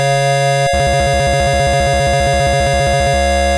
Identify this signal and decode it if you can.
There is no code.